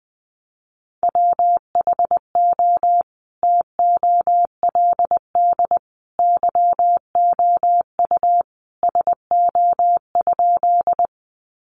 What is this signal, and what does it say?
Morse code, heard as a keyed tone, about 20 wpm, 700 Hz. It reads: WHO TOLD YOU SO?
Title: Sentences from Top 400 Words 20wpm